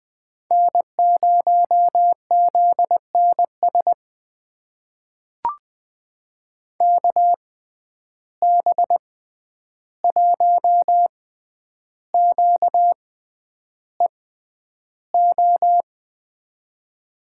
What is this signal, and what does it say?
Morse code, heard as a keyed tone, about 20 wpm, spaced as for 10 wpm, 700 Hz. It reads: N0ZNS KB1QEO
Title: USA Call Signs - 6x Character Spacing - 20wpm